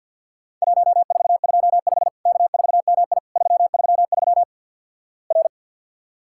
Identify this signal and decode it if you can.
Morse code, heard as a keyed tone, about 48 wpm, 700 Hz. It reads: 1425 K4MI 333 R
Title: Worked All Europe DX Contest - QTC Exchange with Repeats - 50wpm